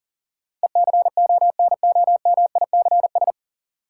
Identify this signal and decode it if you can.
Morse code, heard as a keyed tone, about 40 wpm, 700 Hz. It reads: ECONOMICS
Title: Sentences from Top 1000 Words - Review Part 3 - New Word Drill - 40wpm